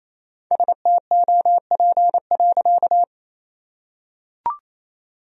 Morse code, heard as a keyed tone, about 28 wpm, 700 Hz. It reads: STOP.
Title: Sentences from Top 1000 Words - Review Part 1 - 28wpm